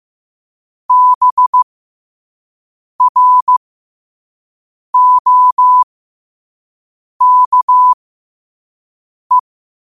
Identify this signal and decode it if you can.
Morse code, heard as a keyed tone, about 15 wpm, spaced as for 5 wpm, 1000 Hz. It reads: BROKE
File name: qotdfiles / 5wpm